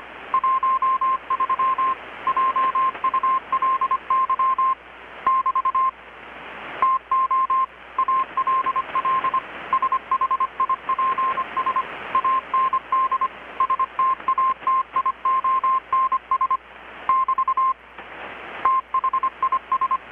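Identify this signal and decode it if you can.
Morse code, heard as a keyed tone, about 25 wpm, 1050 Hz. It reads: 13 JULY <BT> TO ALL SHIPS AND STATIONS <BT> THIS
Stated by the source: Morse from KPH's Night of Nights XX maritime radiotelegraphy event, received in Chiapas, Mexico about 3,800 km away on a Kenwood R-600 with a grounded 10 m long-wire antenna.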